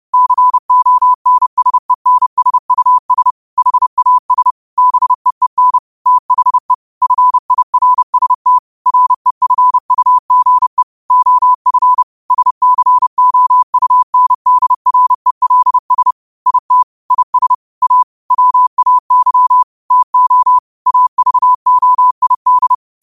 Morse code, heard as a keyed tone, about 30 wpm, 1000 Hz. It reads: CONSENSUS HAS BEEN THE FIRST REFUGE OF SCOUNDRELS IT IS A WAY TO AVOID
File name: qotdfiles / 30wpm